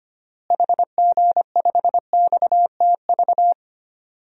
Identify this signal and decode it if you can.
Morse code, heard as a keyed tone, about 25 wpm, 700 Hz. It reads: HG5XTV
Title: DXCC-100 Call Signs - 6 Characters - Repeated 2x - 25wpm